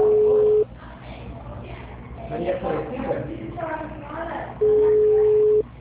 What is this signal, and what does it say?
Ringback tone (Europe, 425 Hz), with speech.